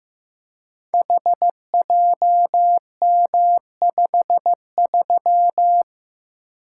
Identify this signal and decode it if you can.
Morse code, heard as a keyed tone, about 15 wpm, 700 Hz. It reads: HJM53
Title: Code Groups - 5 Letter:Numbers - 15wpm